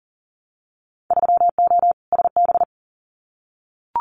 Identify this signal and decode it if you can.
Morse code, heard as a keyed tone, about 37 wpm, 700 Hz. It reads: 3O SB
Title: Winter Field Day Exchanges Encore 40wpm